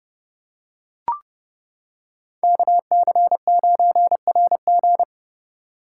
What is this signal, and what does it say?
Morse code, heard as a keyed tone, about 30 wpm, 700 Hz. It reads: KC9RG